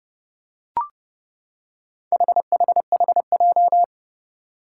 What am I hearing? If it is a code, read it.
Morse code, heard as a keyed tone, about 28 wpm, 700 Hz. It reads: HHHJ
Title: Four Challenging Letter-Number Combinations 30wpm